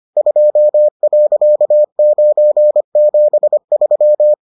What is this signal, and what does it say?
Morse code, heard as a keyed tone, about 25 wpm, 600 Hz. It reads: 2.973